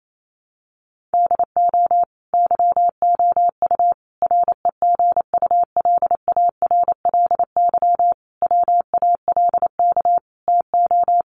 Morse code, heard as a keyed tone, about 28 wpm, 700 Hz. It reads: DO YOU REGULARLY WALK TO
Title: Sentences from Top 1400 Words 28wpm